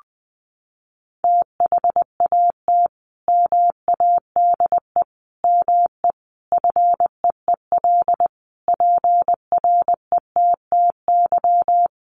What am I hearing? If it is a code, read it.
Morse code, heard as a keyed tone, about 20 wpm, 700 Hz. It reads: THAT MADE ME FEEL PRETTY
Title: Sentences from Top 1000 Words 20wpm